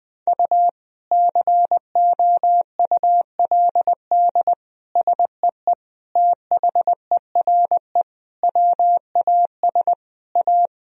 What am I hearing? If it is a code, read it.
Morse code, heard as a keyed tone, about 21 wpm, 700 Hz. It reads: U COULD SEE THERE WAS A
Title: Sentences from Top 500 Words and Abbreviations 20wpm